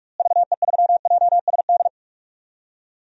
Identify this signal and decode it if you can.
Morse code, heard as a keyed tone, about 46 wpm, 700 Hz. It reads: VE3JSD
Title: DXCC-100 Call Signs - 6 Characters - Repeated 3x - 45wpm